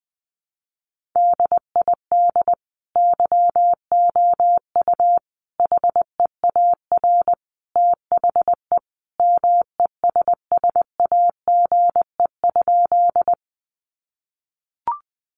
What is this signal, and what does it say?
Morse code, heard as a keyed tone, about 20 wpm, 700 Hz. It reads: DID YOU HEAR THE MESSAGE?
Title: Sentences from Top 1100 Words 20wpm